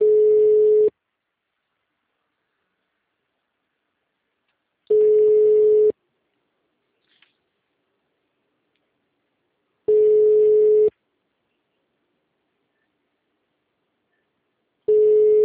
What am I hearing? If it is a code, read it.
Ringback tone (Europe, 425 Hz).
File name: out- -2011- - - .189